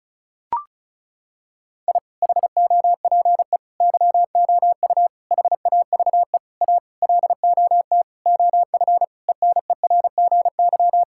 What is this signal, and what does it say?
Morse code, heard as a keyed tone, about 36 wpm, 700 Hz. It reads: I HOPE YOU HAVE A LOT OF ENERGY
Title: Sentences from Top 1000 Words - Review Part 6 - 35wpm